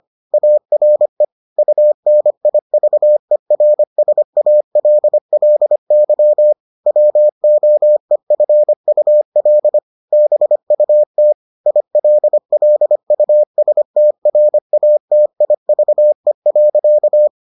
Morse code, heard as a keyed tone, about 25 wpm, 600 Hz. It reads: ARE UNIVERSALLY WOEFUL BUT ILLUSTRATIVE.